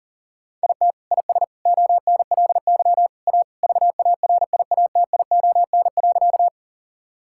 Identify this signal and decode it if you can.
Morse code, heard as a keyed tone, about 40 wpm, 700 Hz. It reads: IT IS ONLY A VARIATION.
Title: Sentences from Top 1600 Words 40wpm